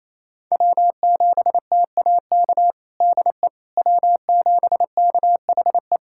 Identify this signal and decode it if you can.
Morse code, heard as a keyed tone, about 28 wpm, 700 Hz. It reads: W7TAK DE W7KHE
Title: USA Callsign DE Callsign - 5 Characters - 28wpm